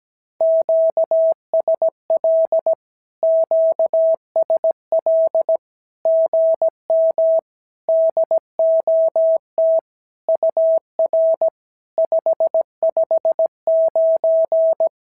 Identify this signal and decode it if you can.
Morse code, heard as a keyed tone, about 17 wpm, 650 Hz. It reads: QSL QSL GM DOT UR 559
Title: SOTA QSO Practice 17wpm